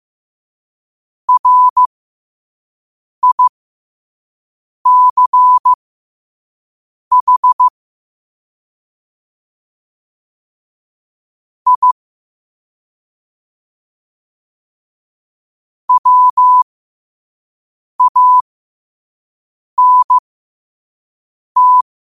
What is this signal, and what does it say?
Morse code, heard as a keyed tone, about 15 wpm, spaced as for 5 wpm, 1000 Hz. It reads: RICH I WANT
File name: qotdfiles / 5wpm